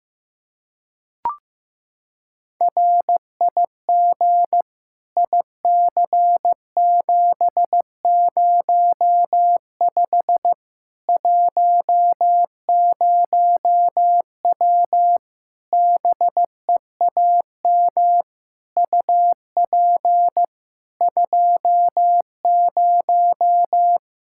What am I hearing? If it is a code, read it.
Morse code, heard as a keyed tone, about 15 wpm, 700 Hz. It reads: RIG IC705 10W BEAM UP 20
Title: Practice QSO - Exchange 2 - Info - 15wpm